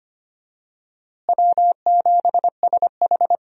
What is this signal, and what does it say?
Morse code, heard as a keyed tone, about 25 wpm, 700 Hz. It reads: W7SH
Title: USA Call Signs - 4 Characters - Repeated 3x - 25wpm